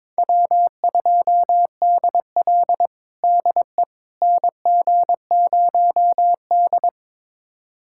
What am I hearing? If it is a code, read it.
Morse code, heard as a keyed tone, about 22 wpm, 700 Hz. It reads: W2DL DE NG0D
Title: USA Callsign DE Callsign - 4 Characters - 22wpm